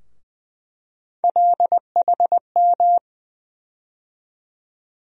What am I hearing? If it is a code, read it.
Morse code, heard as a keyed tone, about 20 wpm, 700 Hz. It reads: LHM
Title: USA Call Sign Postfixes - 3 Letters 20wpm